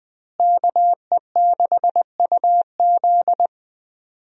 Morse code, heard as a keyed tone, about 20 wpm, 700 Hz. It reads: KE6UZ
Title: USA Call Signs - 5 Characters - Repeated 3x - 20wpm